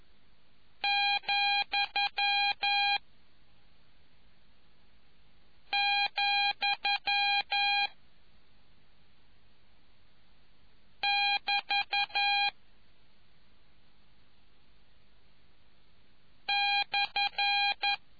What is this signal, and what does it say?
Morse code, heard as a keyed tone, about 11 wpm, 2450 Hz. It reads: , , <BT> /